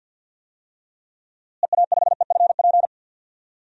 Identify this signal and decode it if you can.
Morse code, heard as a keyed tone, about 53 wpm, 700 Hz. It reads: EA5EFP
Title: DXCC-100 Call Signs - 6 Characters - Repeated 2x - 50wpm